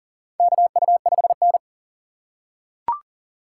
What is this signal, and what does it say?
Morse code, heard as a keyed tone, about 40 wpm, 700 Hz. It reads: KU5N
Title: USA Call Signs - 4 Characters - 8x Character Spacing - 40wpm